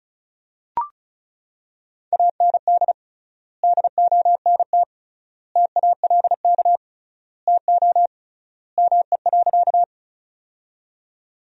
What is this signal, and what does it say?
Morse code, heard as a keyed tone, about 36 wpm, 700 Hz. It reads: AND DONT TALK TO ME.
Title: 3x Word-Spacing - Sentences from Top 400 Words - 35wpm